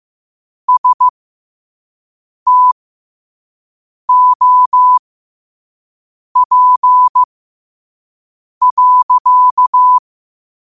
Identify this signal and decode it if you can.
Morse code, heard as a keyed tone, about 15 wpm, spaced as for 5 wpm, 1000 Hz. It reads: STOP.
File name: qotdfiles / 5wpm